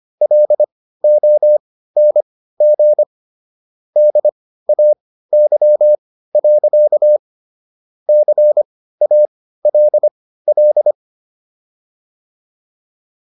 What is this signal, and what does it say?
Morse code, heard as a keyed tone, about 25 wpm, spaced as for 15 wpm, 600 Hz. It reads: LONG DAY. CALL